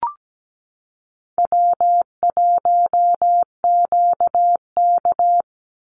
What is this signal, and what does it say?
Morse code, heard as a keyed tone, about 17 wpm, 700 Hz. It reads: W1QK
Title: Easy Call Signs Repeated 3x - 17wpm